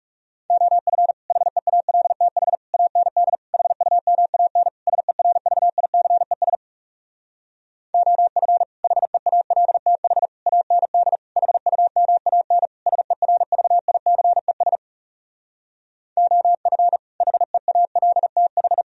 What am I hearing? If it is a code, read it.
Morse code, heard as a keyed tone, about 40 wpm, 700 Hz. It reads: OF HEALTH AND HUMAN SERVICES OF HEALTH AND HUMAN SERVICES OF HEALTH